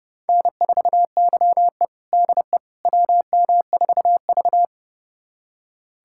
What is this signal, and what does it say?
Morse code, heard as a keyed tone, about 32 wpm, 700 Hz. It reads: N4YE DE WM4V
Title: USA Callsign DE Callsign - 4 Characters - 30wpm